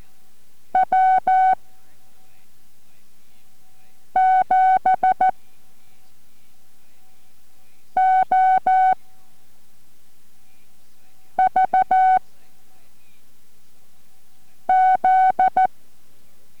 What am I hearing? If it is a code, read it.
Morse code, heard as a keyed tone, about 14 wpm, 750 Hz. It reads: W 7 O V Z